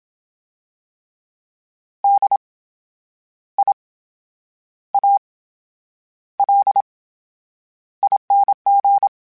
Morse code, heard as a keyed tone, about 27 wpm, spaced as for 9 wpm, 800 Hz. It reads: DIALING
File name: Special INGs / Special INGs-mp3s / VerySlowToFaster Ing @25